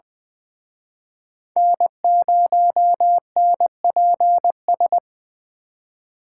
Morse code, heard as a keyed tone, about 24 wpm, 700 Hz. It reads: N0NPS
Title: Speed Racing - USA Call Signs 20wpm